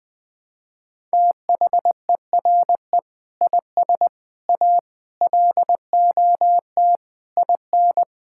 Morse code, heard as a keyed tone, about 20 wpm, 700 Hz. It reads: THERE IS A LOT IN
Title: Sentences from Top 1000 Words - Review Part 5 - 20wpm